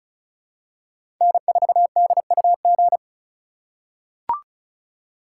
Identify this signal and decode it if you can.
Morse code, heard as a keyed tone, about 34 wpm, 700 Hz. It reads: N4DUG
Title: USA Call Signs - 5 Characters - 2x Character Spacing - 35wpm